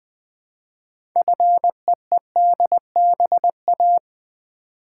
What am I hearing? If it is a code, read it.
Morse code, heard as a keyed tone, about 20 wpm, 700 Hz. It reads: FEEDBA
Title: Build A Word - 2 Letters At A Time - 8-Letter Words - 20wpm